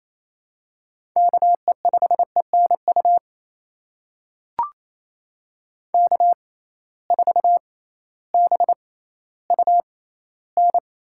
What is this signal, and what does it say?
Morse code, heard as a keyed tone, about 28 wpm, spaced as for 15 wpm, 700 Hz. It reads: KE5ENU K4BUN
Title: USA Call Signs - 6x Character Spacing - 28wpm